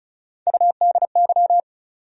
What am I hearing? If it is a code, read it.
Morse code, heard as a keyed tone, about 36 wpm, 700 Hz. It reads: UDY
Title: USA Call Sign Postfixes - 3 Letters 35wpm